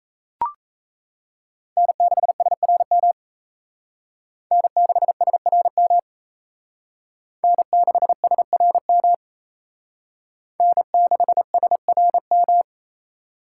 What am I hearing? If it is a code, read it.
Morse code, heard as a keyed tone, about 35 wpm, 700 Hz. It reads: N6SRM N6SRM N6SRM N6SRM
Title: Speed Racing - USA 1x3 Call Signs 28wpm